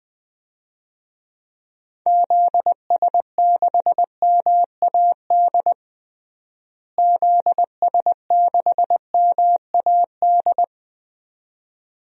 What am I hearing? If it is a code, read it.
Morse code, heard as a keyed tone, about 19 wpm, 700 Hz. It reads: ZS6MAD ZS6MAD
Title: DXCC-100 Call Signs - 6 Characters - Repeated 2x - 20wpm